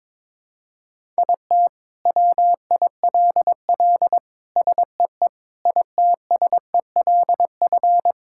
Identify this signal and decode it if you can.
Morse code, heard as a keyed tone, about 22 wpm, 700 Hz. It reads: IT WILL SEE ITSELF
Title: Sentences from Top 1000 Words - Review Part 5 - 22wpm